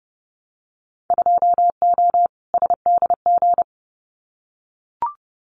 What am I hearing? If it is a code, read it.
Morse code, heard as a keyed tone, about 30 wpm, 700 Hz. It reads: 2O SDG
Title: Winter Field Day Exchanges 30wpm